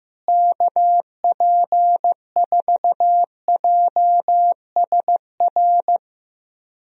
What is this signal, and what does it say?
Morse code, heard as a keyed tone, about 15 wpm, 700 Hz. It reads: KP4JSR
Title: USA Call Signs - 6 Characters - Repeated 2x - 15wpm